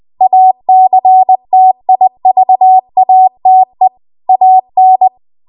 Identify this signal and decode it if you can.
Morse code, heard as a keyed tone, about 20 wpm, 750 Hz. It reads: ACTIVATE AN
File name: Archive / 20 WPM / 20WPM